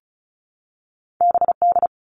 Morse code, heard as a keyed tone, about 36 wpm, 700 Hz. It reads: BD
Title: All Cut Numbers - 2-Digit 35wpm